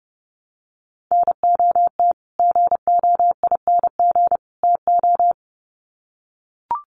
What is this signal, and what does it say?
Morse code, heard as a keyed tone, about 30 wpm, 700 Hz. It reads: NOT GOING TO
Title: Sets of 3 Words 30wpm